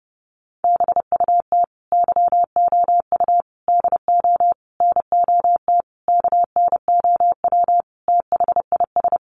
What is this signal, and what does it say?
Morse code, heard as a keyed tone, about 30 wpm, 700 Hz. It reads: BUT YOU DO NOT KNOW THIS
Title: Sentences from Top 600 Words 30wpm